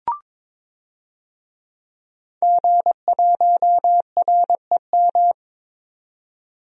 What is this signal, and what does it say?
Morse code, heard as a keyed tone, about 22 wpm, 700 Hz. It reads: G1REM
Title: DXCC-100 Call Signs - 5 Characters - 22wpm